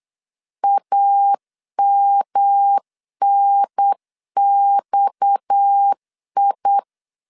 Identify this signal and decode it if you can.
Morse code, heard as a keyed tone, about 8 wpm, 800 Hz. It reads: AMNXI
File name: LEZIONI / 6 PKX0 / 6 PKX 8wpm 3041